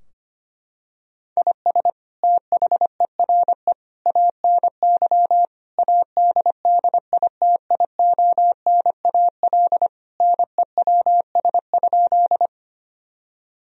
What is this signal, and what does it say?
Morse code, heard as a keyed tone, about 25 wpm, 700 Hz. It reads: IS THERE ANY ADDITIONAL NEWS?
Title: Sentences from Top 1000 Words - Review Part 4 - 25wpm